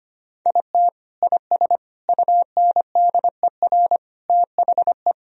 Morse code, heard as a keyed tone, about 25 wpm, 700 Hz. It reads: IT IS UNDER THE
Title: Sentences from Top 1000 Words - Review Part 1 - 25wpm